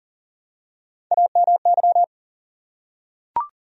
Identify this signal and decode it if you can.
Morse code, heard as a keyed tone, about 40 wpm, 700 Hz. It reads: AMY
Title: USA Common Names - Max 4 Letters - 40wpm